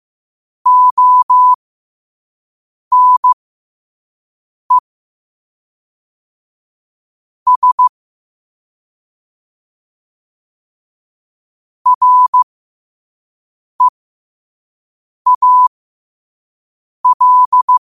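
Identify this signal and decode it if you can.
Morse code, heard as a keyed tone, about 15 wpm, spaced as for 5 wpm, 1000 Hz. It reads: ONE S REAL
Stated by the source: Morse code sent at 5 wpm.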